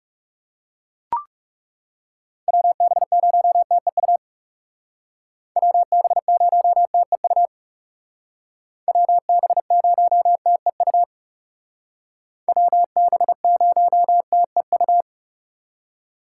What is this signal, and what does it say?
Morse code, heard as a keyed tone, about 35 wpm, 700 Hz. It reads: WB0TEU WB0TEU WB0TEU WB0TEU
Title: Speed Racing - USA Call Signs 30wpm